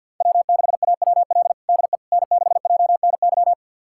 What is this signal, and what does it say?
Morse code, heard as a keyed tone, about 52 wpm, 690 Hz. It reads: W6AWL DE N6JNY
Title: USA Callsign DE Callsign - 5 Characters - 50wpm